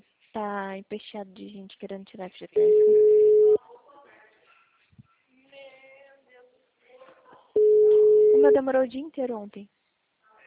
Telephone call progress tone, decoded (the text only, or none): ringback tone